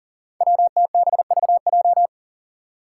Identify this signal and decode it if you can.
Morse code, heard as a keyed tone, about 38 wpm, 700 Hz. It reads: WTBVJ
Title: Five Challenging Letter-Number Combinations 40wpm